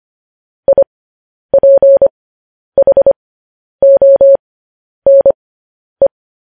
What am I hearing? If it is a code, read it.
Morse code, heard as a keyed tone, about 25 wpm, spaced as for 10 wpm, 550 Hz. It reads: IPHONE